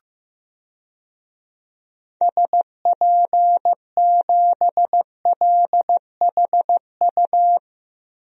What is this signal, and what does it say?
Morse code, heard as a keyed tone, about 15 wpm, 700 Hz. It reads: SP7LHU
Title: DXCC-100 Call Signs - 6 Characters - Repeated 3x - 15wpm